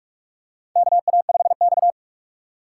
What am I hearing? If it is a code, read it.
Morse code, heard as a keyed tone, about 45 wpm, 700 Hz. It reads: KA5X
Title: USA Call Signs - 4 Characters - 3x Character Spacing - 45wpm